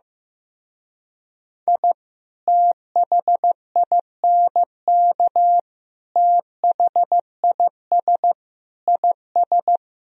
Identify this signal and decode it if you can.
Morse code, heard as a keyed tone, about 16 wpm, 700 Hz. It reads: I THINK THIS IS
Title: Speed-Racing - Sets of 4 Words 15wpm